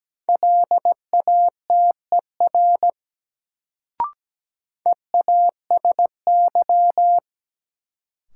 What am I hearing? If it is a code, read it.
Morse code, heard as a keyed tone, about 17 wpm, 700 Hz. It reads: LATER EASY
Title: Sentences from Top 700 Words - New Word Drill - 17wpm